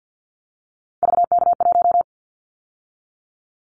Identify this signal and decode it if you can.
Morse code, heard as a keyed tone, about 50 wpm, 700 Hz. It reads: VK1
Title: Callsign Prefixes for Top 50 DXCC - 50wpm